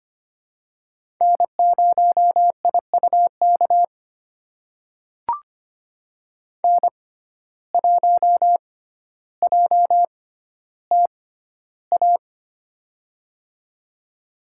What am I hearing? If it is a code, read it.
Morse code, heard as a keyed tone, about 25 wpm, spaced as for 13 wpm, 700 Hz. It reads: N0IUK N1JTA
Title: USA Call Signs - 5 Characters - 6x Character Spacing - 25wpm